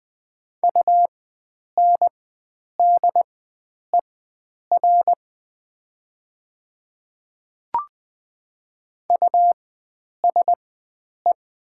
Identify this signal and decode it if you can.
Morse code, heard as a keyed tone, about 20 wpm, spaced as for 9 wpm, 700 Hz. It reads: UNDER USE
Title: 4x Character Spacing - Top 300 Words - 20wpm